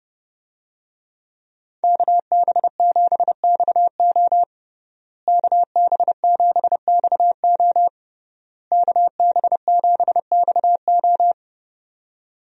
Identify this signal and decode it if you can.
Morse code, heard as a keyed tone, about 30 wpm, 700 Hz. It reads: KB7XO KB7XO KB7XO
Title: USA Call Signs - 5 Characters - Repeated 3x - 30wpm